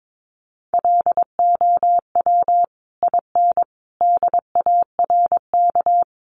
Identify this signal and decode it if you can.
Morse code, heard as a keyed tone, about 22 wpm, 700 Hz. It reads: LOW IN DARK
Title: Sentences from Top 900 Words 22wpm